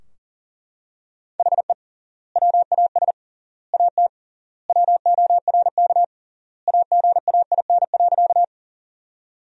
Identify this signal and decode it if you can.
Morse code, heard as a keyed tone, about 37 wpm, 700 Hz. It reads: HE WAS AT WORK AGAIN.
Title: 3x Word-Spacing - Sentences from Top 200 Words - 40wpm